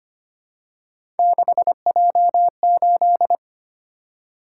Morse code, heard as a keyed tone, about 25 wpm, 700 Hz. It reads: 6J8
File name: Three Letter-Number Combinations - 25wpm